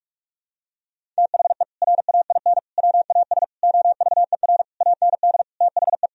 Morse code, heard as a keyed tone, about 46 wpm, 700 Hz. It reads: THE RAIN WAS OVER AND THE
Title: Sentences from Top 1000 Words - Review Part 1 - 45wpm